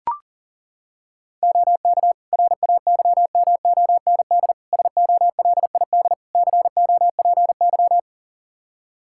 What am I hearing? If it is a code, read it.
Morse code, heard as a keyed tone, about 40 wpm, 700 Hz. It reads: OK RAYMOND SOLID COPY
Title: Practice QSO - Exchange 3 - More Info - 40wpm